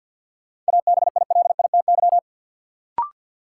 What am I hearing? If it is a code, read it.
Morse code, heard as a keyed tone, about 46 wpm, 700 Hz. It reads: ABILITY
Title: Sentences from Top 1000 Words - Review Part 3 - New Word Drill - 50wpm